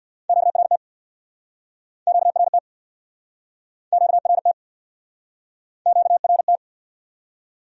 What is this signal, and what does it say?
Morse code, heard as a keyed tone, about 70 wpm, 700 Hz. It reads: QRT QRT QRT QRT